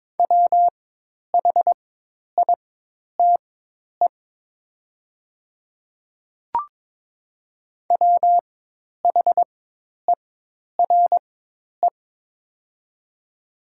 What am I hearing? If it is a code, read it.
Morse code, heard as a keyed tone, about 22 wpm, spaced as for 10 wpm, 700 Hz. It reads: WHITE WHERE
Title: 4x Character Spacing - Top 500 Words - 22wpm